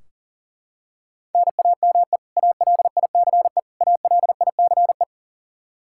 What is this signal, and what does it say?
Morse code, heard as a keyed tone, about 40 wpm, 700 Hz. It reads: NAME ALICE ALICE
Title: Practice QSO - Exchange 1 - Beginning - 40wpm